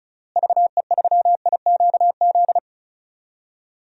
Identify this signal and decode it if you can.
Morse code, heard as a keyed tone, about 36 wpm, 700 Hz. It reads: VE3IQZ
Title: DXCC-100 Call Signs - 6 Characters - Repeated 3x - 35wpm